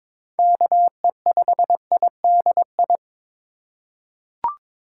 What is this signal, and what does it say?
Morse code, heard as a keyed tone, about 22 wpm, 700 Hz. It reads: KE5IDI
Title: USA Call Signs - 6 Characters - 3x Character Spacing - 22wpm